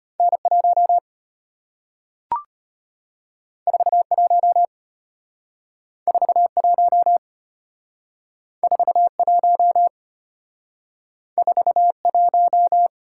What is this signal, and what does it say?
Morse code, heard as a keyed tone, about 33 wpm, 690 Hz. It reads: N1 41 41 41 41